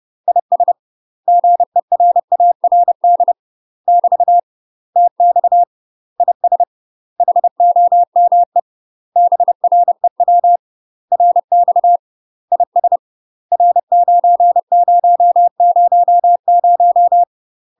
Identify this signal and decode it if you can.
Morse code, heard as a keyed tone, about 30 wpm, 700 Hz. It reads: IS GERARD <BT> TX IS HOME BREW RX IS R9000
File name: UK MASTER QSOs TRAINING / 30WPM / UK MASTER QSOs 0003